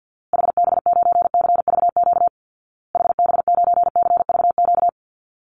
Morse code, heard as a keyed tone, about 56 wpm, 700 Hz. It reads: HB9CVQ HB9CVQ
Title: Call Signs Repeated 2x - 50wpm